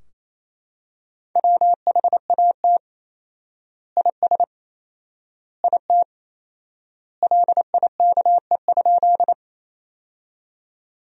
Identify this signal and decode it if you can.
Morse code, heard as a keyed tone, about 28 wpm, 700 Hz. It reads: WHAT IS IT LIKE?
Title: 4x Word-Spacing - Sentences from Top 100 Words - 28wpm